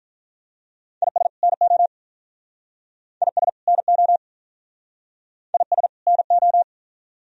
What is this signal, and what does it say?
Morse code, heard as a keyed tone, about 45 wpm, 700 Hz. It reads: IS NO IS NO IS NO